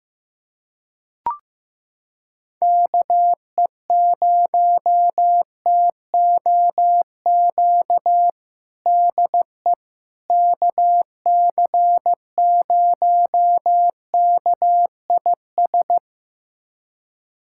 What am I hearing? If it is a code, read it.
Morse code, heard as a keyed tone, about 15 wpm, 700 Hz. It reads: KE0TOQ DE KC0KIS
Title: USA Callsign DE Callsign - 6 Characters - 15wpm